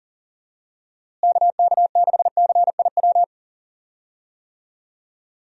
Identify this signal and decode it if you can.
Morse code, heard as a keyed tone, about 37 wpm, 700 Hz. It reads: KK6CIW
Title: USA Call Signs - 6 Characters - Repeated 2x - 40wpm